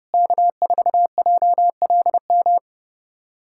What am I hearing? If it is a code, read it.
Morse code, heard as a keyed tone, about 30 wpm, 700 Hz. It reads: K4JLM
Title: USA Callsign DE Callsign - 5 Characters - 30wpm